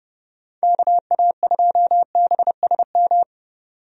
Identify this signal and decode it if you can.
Morse code, heard as a keyed tone, about 30 wpm, 700 Hz. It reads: KA2BSM
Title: USA Call Signs - 6 Characters - 30wpm